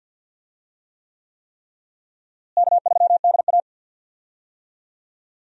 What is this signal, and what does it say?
Morse code, heard as a keyed tone, about 52 wpm, 700 Hz. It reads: K3DA
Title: USA Call Signs - 4 Characters - Repeated 3x - 50wpm